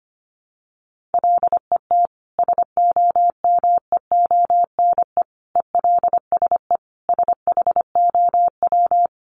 Morse code, heard as a keyed tone, about 25 wpm, 700 Hz. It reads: LET SOMEONE ELSE SHOW
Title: Sentences from Top 1200 Words 25wpm